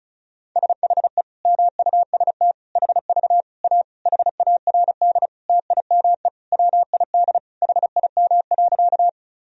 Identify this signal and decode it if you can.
Morse code, heard as a keyed tone, about 34 wpm, 700 Hz. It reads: SHE MUST HV A HARD TIME WID HIM.
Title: Sentences from Top 300 Words and Abbreviations 35wpm